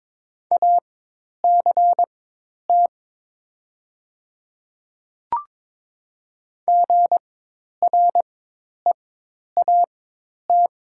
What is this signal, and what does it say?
Morse code, heard as a keyed tone, about 22 wpm, spaced as for 10 wpm, 700 Hz. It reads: ACT GREAT
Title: 4x Character Spacing - Top 500 Words - 22wpm